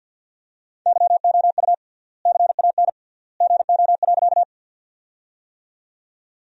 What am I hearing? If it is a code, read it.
Morse code, heard as a keyed tone, about 50 wpm, 700 Hz. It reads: YOU CAN GO.
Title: 3x Word-Spacing - Sentences from Top 100 Words - 50wpm